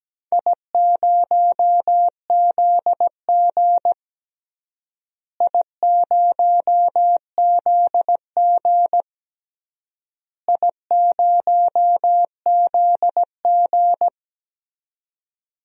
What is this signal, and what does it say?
Morse code, heard as a keyed tone, about 17 wpm, 700 Hz. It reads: I0ZG I0ZG I0ZG
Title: DXCC-100 Call Signs - 4 Characters - Repeated 3x - 17wpm